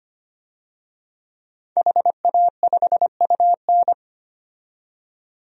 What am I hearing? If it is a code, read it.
Morse code, heard as a keyed tone, about 25 wpm, 700 Hz. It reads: HA5UN
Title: DXCC-100 Call Signs - 5 Characters - 25wpm